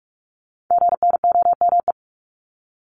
Morse code, heard as a keyed tone, about 46 wpm, 700 Hz. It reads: GNOME
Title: Multi-letter Phonemes 45wpm